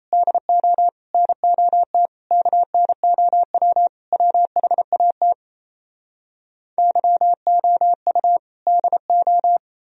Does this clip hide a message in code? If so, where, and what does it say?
Yes: Morse code from 0.1 to 9.6 s, heard as a keyed tone, about 38 wpm, 700 Hz. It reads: DO NOT KNOW WHAT YOU DO